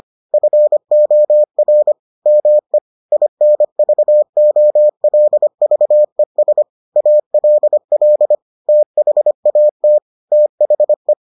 Morse code, heard as a keyed tone, about 25 wpm, 600 Hz. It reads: FOR ME INVOLVES ALL THAT THE